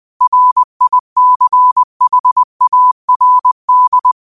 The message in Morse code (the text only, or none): RICHARD